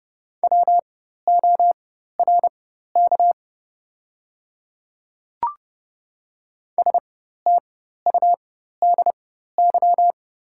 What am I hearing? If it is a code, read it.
Morse code, heard as a keyed tone, about 30 wpm, spaced as for 14 wpm, 700 Hz. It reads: WORK STUDY
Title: 4x Character Spacing - Top 300 Words - 30wpm